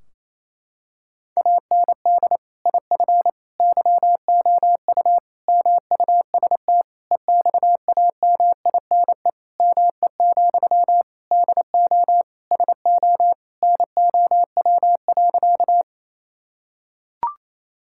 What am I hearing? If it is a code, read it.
Morse code, heard as a keyed tone, about 28 wpm, 700 Hz. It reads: AND IF YOU MUST EXAMINE ME, DO SO NOW.
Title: Sentences from Top 1600 Words 28wpm